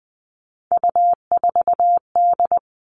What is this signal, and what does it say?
Morse code, heard as a keyed tone, about 20 wpm, 700 Hz. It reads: U4D
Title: All Cut Numbers - 3-Digit 20wpm